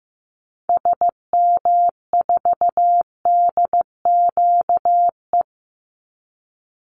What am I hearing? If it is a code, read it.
Morse code, heard as a keyed tone, about 15 wpm, 700 Hz. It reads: SM4DQE
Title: Call Signs Repeated 3x - 15wpm